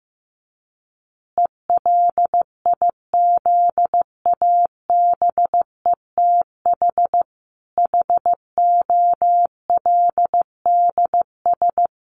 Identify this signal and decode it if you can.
Morse code, heard as a keyed tone, about 15 wpm, 700 Hz. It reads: ELIZABETH HOLDS
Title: Easy Phrases 15wpm